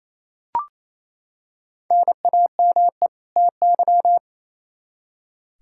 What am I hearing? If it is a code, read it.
Morse code, heard as a keyed tone, about 28 wpm, 700 Hz. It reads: NAME TY
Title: Practice QSO - Exchange 1 - Beginning - 28wpm